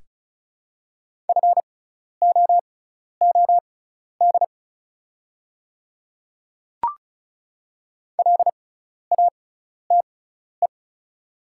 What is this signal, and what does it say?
Morse code, heard as a keyed tone, about 36 wpm, spaced as for 12 wpm, 700 Hz. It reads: FOOD LATE
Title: 6x Character Spacing - Top 300 Words - 35wpm